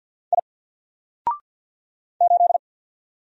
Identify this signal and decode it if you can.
Morse code, heard as a keyed tone, about 50 wpm, 700 Hz. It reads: I 8
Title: Single Letter-Number Alt 50wpm